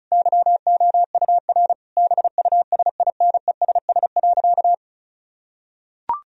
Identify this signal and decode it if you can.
Morse code, heard as a keyed tone, about 36 wpm, 700 Hz. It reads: YOUR BUSINESS.
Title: Sentences from Top 1000 Words - Review Part 4 - 35wpm